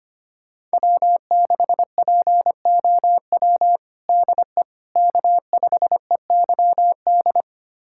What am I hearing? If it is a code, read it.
Morse code, heard as a keyed tone, about 25 wpm, 700 Hz. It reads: W6POW DE K5EYD
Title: USA Callsign DE Callsign 25wpm